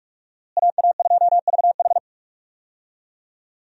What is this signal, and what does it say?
Morse code, heard as a keyed tone, about 41 wpm, 700 Hz. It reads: AA2VH
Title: USA Call Signs - 5 Characters - Repeated 3x - 45wpm